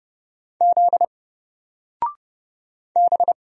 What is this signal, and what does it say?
Morse code, heard as a keyed tone, about 29 wpm, 700 Hz. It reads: Z B
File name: Single-Letter-Number-30wpm